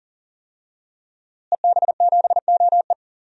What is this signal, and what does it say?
Morse code, heard as a keyed tone, about 40 wpm, 700 Hz. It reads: EB7OE
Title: DXCC-100 Call Signs - 5 Characters - Repeated 2x - 40wpm